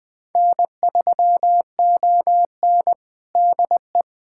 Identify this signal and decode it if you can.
Morse code, heard as a keyed tone, about 21 wpm, 700 Hz. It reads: N3ON DE
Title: USA Callsign DE Callsign - 4 Characters - 20wpm